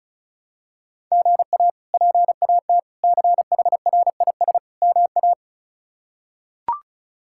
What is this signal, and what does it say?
Morse code, heard as a keyed tone, about 36 wpm, 700 Hz. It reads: GA PAT CHRIS MA
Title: K1USN SST QSO Practice - 35wpm